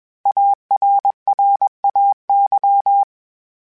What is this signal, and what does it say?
Morse code, heard as a keyed tone, about 21 wpm, 800 Hz. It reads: ARRAY